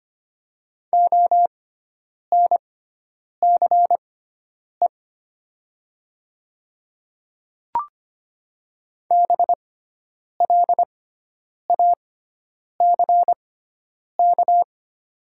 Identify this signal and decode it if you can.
Morse code, heard as a keyed tone, about 25 wpm, spaced as for 9 wpm, 700 Hz. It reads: ONCE BLACK
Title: 6x Character Spacing - Top 500 Words - 25wpm